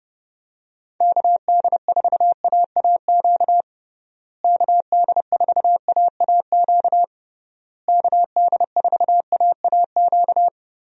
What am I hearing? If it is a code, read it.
Morse code, heard as a keyed tone, about 30 wpm, 700 Hz. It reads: KD4AAQ KD4AAQ KD4AAQ
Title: USA Call Signs - 6 Characters - Repeated 3x - 30wpm